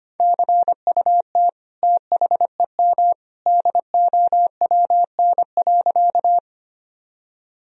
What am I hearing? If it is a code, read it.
Morse code, heard as a keyed tone, about 25 wpm, 700 Hz. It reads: CUT THEM DOWN.
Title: Sentences from Top 1000 Words - Review Part 1 - 25wpm